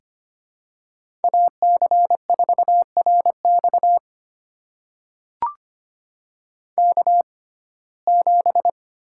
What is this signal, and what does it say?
Morse code, heard as a keyed tone, about 25 wpm, spaced as for 18 wpm, 700 Hz. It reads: AC4RX K7
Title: USA Call Signs - 5 Characters - 6x Character Spacing - 25wpm